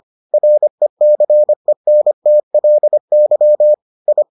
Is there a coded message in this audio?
Yes: Morse code from 0.3 to 4.2 s, heard as a keyed tone, about 25 wpm, 600 Hz. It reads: RECENTLY I